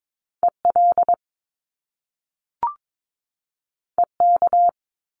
Morse code, heard as a keyed tone, about 22 wpm, 700 Hz. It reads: EL EK
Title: Top Two Letter Combinations in Words 22wpm